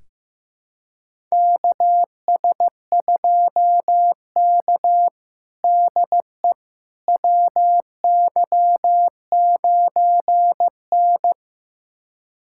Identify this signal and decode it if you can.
Morse code, heard as a keyed tone, about 15 wpm, 700 Hz. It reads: KS2K DE WY9N
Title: USA Callsign DE Callsign - 4 Characters - 15wpm